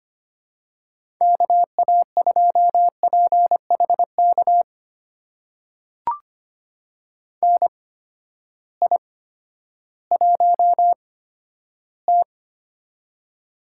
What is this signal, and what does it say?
Morse code, heard as a keyed tone, about 25 wpm, spaced as for 12 wpm, 700 Hz. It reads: KA2PHK NI1T
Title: USA Call Signs - 8x Character Spacing - 25wpm